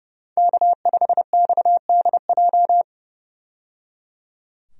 Morse code, heard as a keyed tone, about 30 wpm, 700 Hz. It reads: K5XDJ
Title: USA Call Signs - 5 Characters - 30wpm